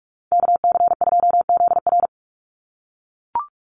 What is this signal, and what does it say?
Morse code, heard as a keyed tone, about 44 wpm, 700 Hz. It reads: KC2ZR
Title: USA Call Signs 45wpm